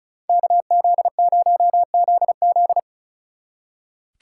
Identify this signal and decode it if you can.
Morse code, heard as a keyed tone, about 40 wpm, 700 Hz. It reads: KZ0ZZ
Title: Speed Racing - USA 2x2 Call Signs 35wpm